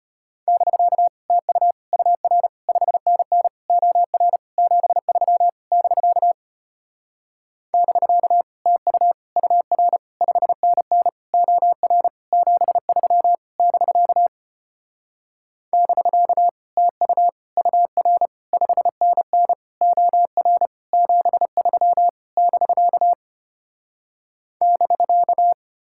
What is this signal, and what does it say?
Morse code, heard as a keyed tone, about 33 wpm, 700 Hz. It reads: <BK> TU UR 5NN OR 73 <BK> <BK> TU UR 5NN OR 73 <BK> <BK> TU UR 5NN OR 73 <BK> <BK>